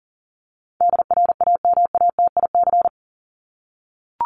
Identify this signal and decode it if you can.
Morse code, heard as a keyed tone, about 40 wpm, 700 Hz. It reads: DRAMATIC
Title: Top 1251-1500 Words 40wpm